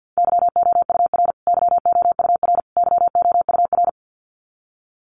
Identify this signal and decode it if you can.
Morse code, heard as a keyed tone, about 50 wpm, 700 Hz. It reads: YOUR YOUR YOUR
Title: Top 500 Words Repeated 50wpm